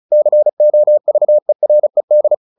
Morse code, heard as a keyed tone, about 35 wpm, 600 Hz. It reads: COVERED
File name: the old man and the sea / the old man and the sea 1 35wpm